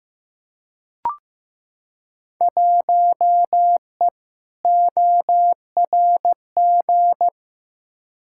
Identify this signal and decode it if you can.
Morse code, heard as a keyed tone, about 15 wpm, 700 Hz. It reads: 1E ORG
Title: Field Day Exchanges - 15wpm